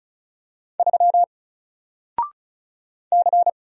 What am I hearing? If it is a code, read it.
Morse code, heard as a keyed tone, about 36 wpm, 700 Hz. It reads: 3 C
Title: Single Letter-Number 35wpm